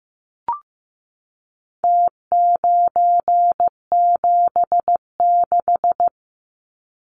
Morse code, heard as a keyed tone, about 15 wpm, 700 Hz. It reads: T976
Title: Numbers - Contest Style - 4-Digit 15wpm